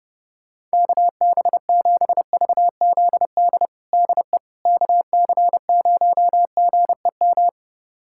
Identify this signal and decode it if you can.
Morse code, heard as a keyed tone, about 32 wpm, 700 Hz. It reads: KB7VZD DE KC0GEM
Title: USA Callsign DE Callsign 30wpm